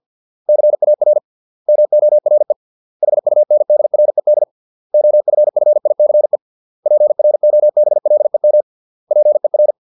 Morse code, heard as a keyed tone, about 50 wpm, 600 Hz. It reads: CAR MORE HUNDRED OFFICE PROBLEM PER